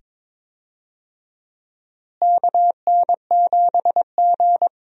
Morse code, heard as a keyed tone, about 22 wpm, 700 Hz. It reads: KN7G
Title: USA Call Signs - 4 Characters - Repeated 2x - 22wpm